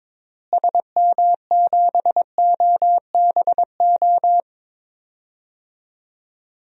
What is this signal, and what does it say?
Morse code, heard as a keyed tone, about 22 wpm, 700 Hz. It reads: SM7OBO
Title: DXCC-100 Call Signs - 6 Characters - Repeated 3x - 22wpm